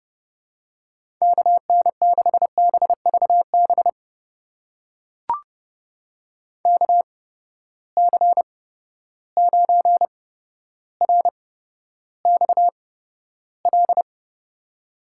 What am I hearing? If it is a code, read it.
Morse code, heard as a keyed tone, about 30 wpm, spaced as for 13 wpm, 700 Hz. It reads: KN6BVB KC9RXL
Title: USA Call Signs - 6 Characters - 8x Character Spacing - 30wpm